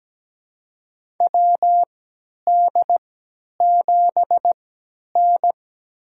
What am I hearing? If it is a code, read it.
Morse code, heard as a keyed tone, about 17 wpm, spaced as for 10 wpm, 700 Hz. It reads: WD7N
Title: USA Call Signs - 3x Character Spacing - 17wpm